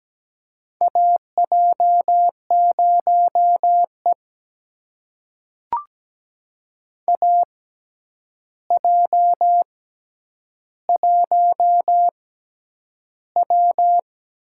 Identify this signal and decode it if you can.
Morse code, heard as a keyed tone, about 17 wpm, spaced as for 9 wpm, 700 Hz. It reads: AJ0E AJ1W
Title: USA Call Signs - 4 Characters - 6x Character Spacing - 17wpm